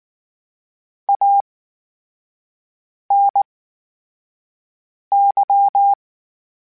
Morse code, heard as a keyed tone, about 19 wpm, spaced as for 5 wpm, 800 Hz. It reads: ANY
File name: Special INGs / Special INGs-mp3s / VerySlowToFaster Ing @18